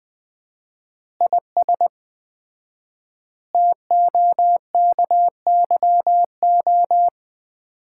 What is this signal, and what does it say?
Morse code, heard as a keyed tone, about 20 wpm, 700 Hz. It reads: IS TOKYO
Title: Common QSO Phrases - 4x Word-Spacing - 20wpm